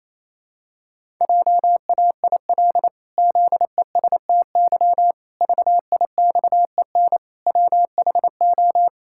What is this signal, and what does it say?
Morse code, heard as a keyed tone, about 27 wpm, 700 Hz. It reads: JAIL ZESTY VIXEN WHO
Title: Pangram Challenge v2 - 28wpm